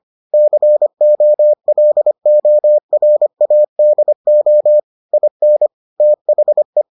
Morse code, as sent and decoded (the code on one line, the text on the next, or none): -.-. --- .-.. --- .-. .- -.. --- / .. -. / - .... .
COLORADO IN THE